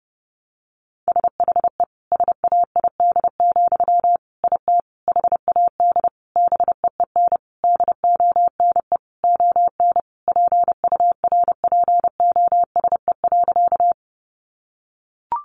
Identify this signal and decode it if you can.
Morse code, heard as a keyed tone, about 30 wpm, 700 Hz. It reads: SHE SAID, IT HAD BEEN DONE ON PURPOSE.
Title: Sentences from Top 800 Words 30wpm